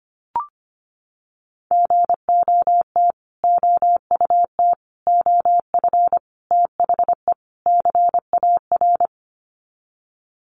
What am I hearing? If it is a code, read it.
Morse code, heard as a keyed tone, about 25 wpm, 700 Hz. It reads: GOT OUT OF THE CAR
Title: Sets of 5 Words Encore 25wpm